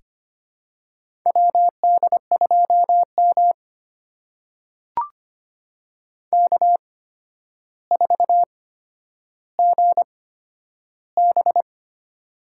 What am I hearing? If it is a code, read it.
Morse code, heard as a keyed tone, about 25 wpm, spaced as for 11 wpm, 700 Hz. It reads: WD2M K4GB
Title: USA Call Signs - 4 Characters - 8x Character Spacing - 25wpm